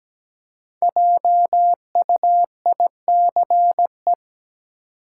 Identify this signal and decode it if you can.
Morse code, heard as a keyed tone, about 17 wpm, 700 Hz. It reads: JUICE
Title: Five Letter Words - 17wpm